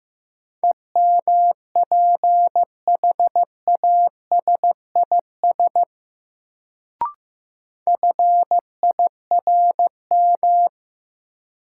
Morse code, heard as a keyed tone, about 15 wpm, 700 Hz. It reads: EMPHASIS FIRM
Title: Sentences from Top 1400 Words - New Word Drill - 15wpm